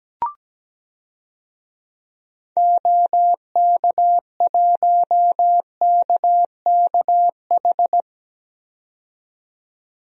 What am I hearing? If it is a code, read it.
Morse code, heard as a keyed tone, about 17 wpm, 700 Hz. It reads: OK1KKH
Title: DXCC-100 Call Signs - 6 Characters - 17wpm